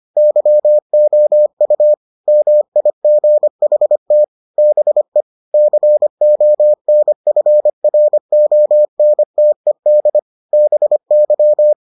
Morse code, heard as a keyed tone, about 25 wpm, 600 Hz. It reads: YOU MIGHT BE CONFRONTED BY